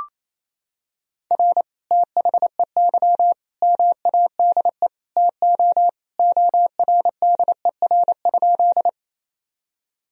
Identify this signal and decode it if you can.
Morse code, heard as a keyed tone, about 28 wpm, 700 Hz. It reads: R THEY MADE TO ORDER?
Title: Sentences from Top 400 Words and Abbreviations 28wpm